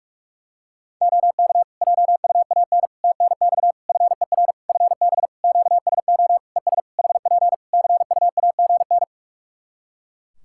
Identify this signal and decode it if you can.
Morse code, heard as a keyed tone, about 46 wpm, 700 Hz. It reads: OK JUAN TNX FER FB QSO ES HP CUAGN
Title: K1USN SST QSO Practice 45wpm